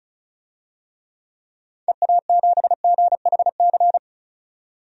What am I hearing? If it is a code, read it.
Morse code, heard as a keyed tone, about 36 wpm, 700 Hz. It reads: EA7GHC
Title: DXCC-100 Call Signs - 6 Characters - 35wpm